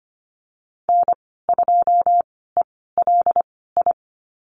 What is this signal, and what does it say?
Morse code, heard as a keyed tone, about 25 wpm, spaced as for 16 wpm, 690 Hz. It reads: N2ELI
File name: Lesson 15g - Number 5 - Call Signs 25wpm